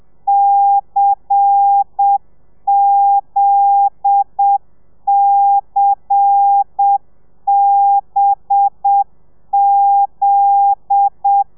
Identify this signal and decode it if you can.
Morse code, heard as a keyed tone, about 7 wpm, 800 Hz. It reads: CZCBZ
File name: telegrafia / i2tfj / Lez10 30